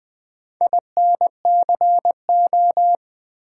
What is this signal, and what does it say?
Morse code, heard as a keyed tone, about 20 wpm, 700 Hz. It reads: INCO
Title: Build A Word - 2 Letters At A Time - 6-Letter Words - 20wpm